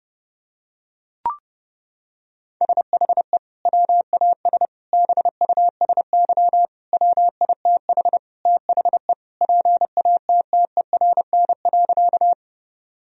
Morse code, heard as a keyed tone, about 30 wpm, 700 Hz. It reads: SHE WAS BUSY WITH THE PATTERN.
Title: Sentences from Top 1000 Words - Review Part 2 - 30wpm